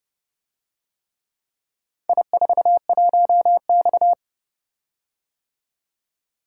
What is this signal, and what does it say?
Morse code, heard as a keyed tone, about 30 wpm, 700 Hz. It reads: I41X